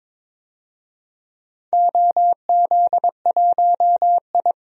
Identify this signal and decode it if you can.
Morse code, heard as a keyed tone, about 22 wpm, 700 Hz. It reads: OZ1I
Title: DXCC-100 Call Signs - 4 Characters - 22wpm